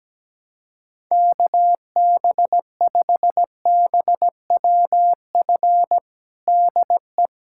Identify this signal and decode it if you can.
Morse code, heard as a keyed tone, about 17 wpm, 700 Hz. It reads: KB5BWF DE
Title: USA Callsign DE Callsign 17wpm